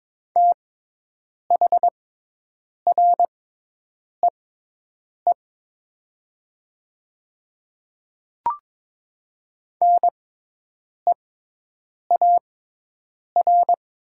Morse code, heard as a keyed tone, about 22 wpm, spaced as for 8 wpm, 700 Hz. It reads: THREE NEAR
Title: 6x Character Spacing - Top 500 Words - 22wpm